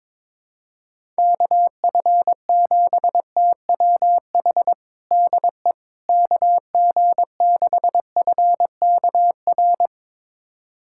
Morse code, heard as a keyed tone, about 22 wpm, 700 Hz. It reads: KF7TWH DE KG6FKR
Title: USA Callsign DE Callsign - 6 Characters - 22wpm